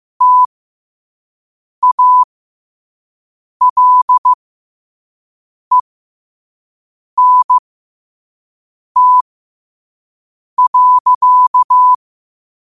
Morse code, heard as a keyed tone, about 15 wpm, spaced as for 5 wpm, 1000 Hz. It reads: TALENT.
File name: qotdfiles / 5wpm